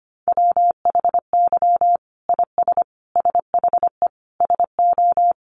Morse code, heard as a keyed tone, about 25 wpm, 700 Hz. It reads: WHY IS SHE SO
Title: Sentences from Top 800 Words 25wpm